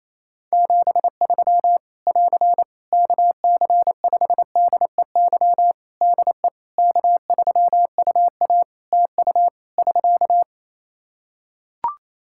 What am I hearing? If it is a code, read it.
Morse code, heard as a keyed tone, about 28 wpm, 700 Hz. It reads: 73 <AR> KC5DEY DE K3UA TU <SK>
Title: Practice QSO - Exchange 4 - Ending - 28wpm